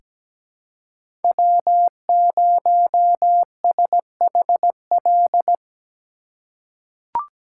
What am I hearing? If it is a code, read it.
Morse code, heard as a keyed tone, about 17 wpm, 700 Hz. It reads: W0SHL
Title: K1USN SST QSO Practice - 17wpm